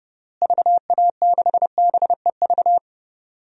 Morse code, heard as a keyed tone, about 32 wpm, 700 Hz. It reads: VA6BEV
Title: DXCC-100 Call Signs - 6 Characters - Repeated 3x - 30wpm